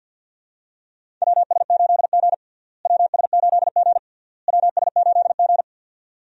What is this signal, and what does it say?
Morse code, heard as a keyed tone, about 48 wpm, 700 Hz. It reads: WS8G WS8G WS8G
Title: USA Call Signs - 4 Characters - Repeated 3x - 50wpm